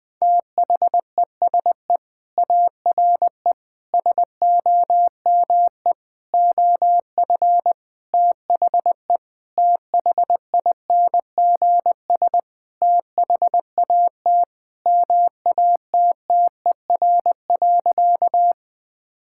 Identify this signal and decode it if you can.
Morse code, heard as a keyed tone, about 21 wpm, 700 Hz. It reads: THESE ARE SOME OF THE THINGS THAT MATTER.
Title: Sentences with 5-Word Phrases - 20wpm